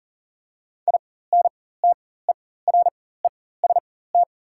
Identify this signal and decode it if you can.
Morse code, heard as a keyed tone, about 40 wpm, spaced as for 19 wpm, 700 Hz. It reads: INTEREST
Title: 4x Character Spacing - Top 500 Words - 40wpm